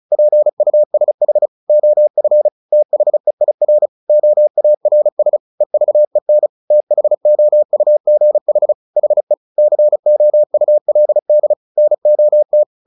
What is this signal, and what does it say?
Morse code, heard as a keyed tone, about 34 wpm, 600 Hz. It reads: PUSH OF THEIR OARS EVEN THOUGH HE COULD NOT
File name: the old man and the sea / the old man and the sea 1 35wpm